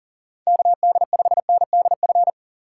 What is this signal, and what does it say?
Morse code, heard as a keyed tone, about 42 wpm, 690 Hz. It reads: KD5NDF
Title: USA Call Signs - 2x Character Spacing - 40wpm